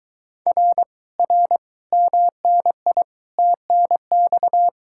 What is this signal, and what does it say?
Morse code, heard as a keyed tone, about 23 wpm, 700 Hz. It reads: R R MNI TNX